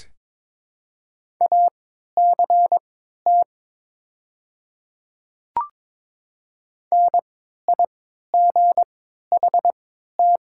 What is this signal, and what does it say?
Morse code, heard as a keyed tone, about 22 wpm, spaced as for 12 wpm, 690 Hz. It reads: ACT NIGHT
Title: 3x Character Spacing - Top 300 Words - 22wpm